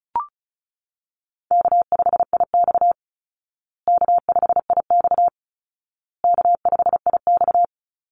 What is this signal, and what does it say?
Morse code, heard as a keyed tone, about 34 wpm, 700 Hz. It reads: K5IX K5IX K5IX
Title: Call Signs Repeated 2x - 35wpm